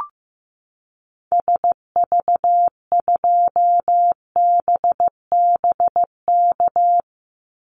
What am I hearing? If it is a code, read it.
Morse code, heard as a keyed tone, about 15 wpm, 700 Hz. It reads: SV2BBK
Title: Call Signs Repeated 2x - 15wpm